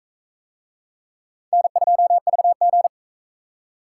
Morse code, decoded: N2VG

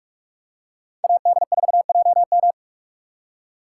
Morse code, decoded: AD4JM

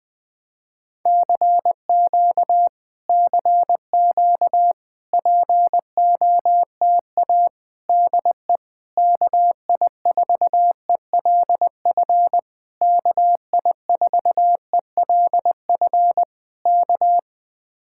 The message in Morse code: CQ CQ POTA DE KI4ELF KI4ELF K

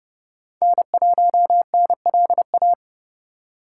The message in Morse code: N1NLA

